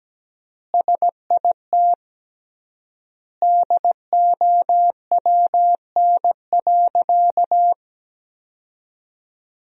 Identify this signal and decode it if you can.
Morse code, heard as a keyed tone, about 17 wpm, 700 Hz. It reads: SIT DOWN.
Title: 3x Word-Spacing - Sentences from Top 500 Words - 17wpm